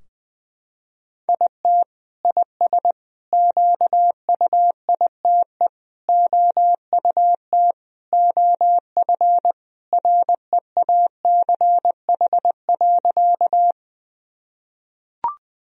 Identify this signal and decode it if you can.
Morse code, heard as a keyed tone, about 21 wpm, 700 Hz. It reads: IT IS QUITE OUT OF REACH.
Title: Sentences from Top 1000 Words - Review Part 2 - 20wpm